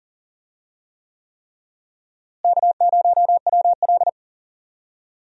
Morse code, heard as a keyed tone, about 37 wpm, 700 Hz. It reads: K0WL
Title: USA Call Signs - 4 Characters - Repeated 3x - 40wpm